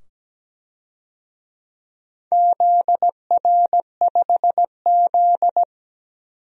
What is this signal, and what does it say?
Morse code, heard as a keyed tone, about 17 wpm, 700 Hz. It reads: ZR5Z
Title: DXCC-100 Call Signs - 4 Characters - 17wpm